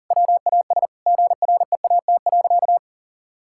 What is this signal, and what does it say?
Morse code, heard as a keyed tone, about 40 wpm, 700 Hz. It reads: WAS GREAT.
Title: Sentences from Top 1600 Words 40wpm